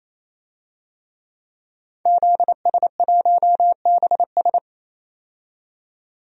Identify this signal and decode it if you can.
Morse code, heard as a keyed tone, about 27 wpm, 700 Hz. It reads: ZS1BS
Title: DXCC-100 Call Signs - 5 Characters - 28wpm